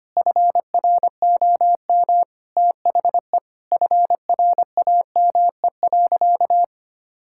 Morse code, heard as a keyed tone, about 25 wpm, 700 Hz. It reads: FROM THE FRAME.
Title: Sentences from Top 1000 Words - Review Part 6 - 25wpm